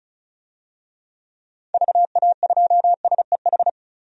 Morse code, heard as a keyed tone, about 36 wpm, 700 Hz. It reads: VA2SEH
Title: DXCC-100 Call Signs - 6 Characters - 35wpm